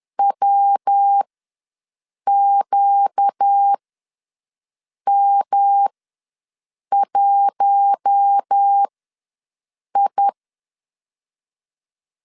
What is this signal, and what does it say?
Morse code, heard as a keyed tone, about 10 wpm, spaced as for 6 wpm, 800 Hz. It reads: WQM1I
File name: LEZIONI / 5 LBVC / 5 L ws 10wpm 3636